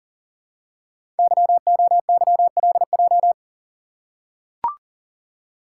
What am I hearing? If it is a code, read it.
Morse code, heard as a keyed tone, about 40 wpm, 700 Hz. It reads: YOYLJ
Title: Code Groups - 5 Letter:Numbers - 40wpm